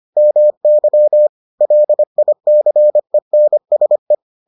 Morse code, heard as a keyed tone, about 25 wpm, 600 Hz. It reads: MY LICENSE